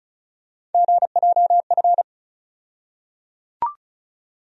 Morse code, heard as a keyed tone, about 34 wpm, 700 Hz. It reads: GJF
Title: USA Call Sign Postfixes - 3 Letters 35wpm